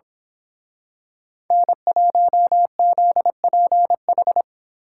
Morse code, heard as a keyed tone, about 26 wpm, 700 Hz. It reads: N1ZPH
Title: Speed Racing - USA 1x3 Call Signs 22wpm